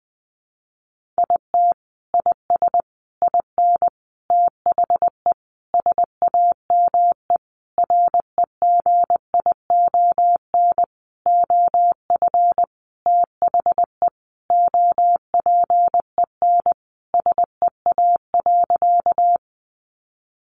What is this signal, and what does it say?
Morse code, heard as a keyed tone, about 20 wpm, 700 Hz. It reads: IT IS IN THE SAME REGION OF THE OPEN SEA.
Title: Sentences from Top 1500 Words 20wpm